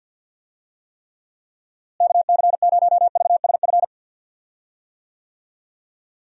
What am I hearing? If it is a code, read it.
Morse code, heard as a keyed tone, about 45 wpm, 700 Hz. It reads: KC0VSF
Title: USA Call Signs - 6 Characters - Repeated 2x - 50wpm